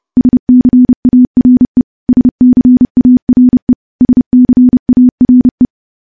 Morse code, heard as a keyed tone, about 31 wpm, 250 Hz. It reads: SCARE SCARE SCARE
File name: morse / morse-training-30WPM